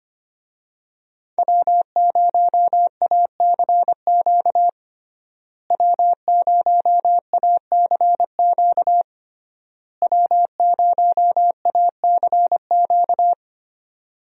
Morse code, heard as a keyed tone, about 25 wpm, 700 Hz. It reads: W0ACQ W0ACQ W0ACQ
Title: USA Call Signs - 5 Characters - Repeated 3x - 25wpm